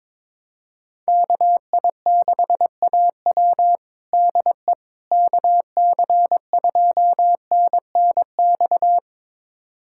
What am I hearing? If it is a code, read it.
Morse code, heard as a keyed tone, about 22 wpm, 700 Hz. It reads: KI6AW DE KC2NNX
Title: USA Callsign DE Callsign 22wpm